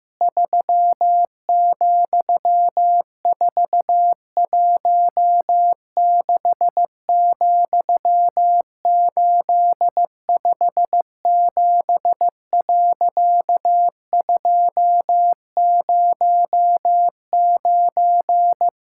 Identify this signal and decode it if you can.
Morse code, heard as a keyed tone, about 15 wpm, 700 Hz. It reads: 3,416,857.209
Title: Pangram Challenge v2 - 15wpm